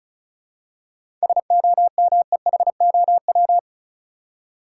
Morse code, heard as a keyed tone, about 36 wpm, 700 Hz. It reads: SOMEHOW
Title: Seven Letter Words - 35wpm